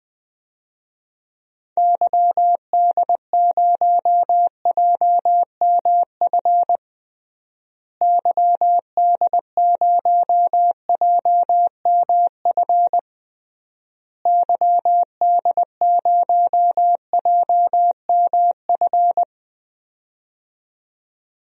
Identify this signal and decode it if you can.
Morse code, heard as a keyed tone, about 20 wpm, 700 Hz. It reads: YD0JMF YD0JMF YD0JMF
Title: DXCC-100 Call Signs - 6 Characters - Repeated 3x - 20wpm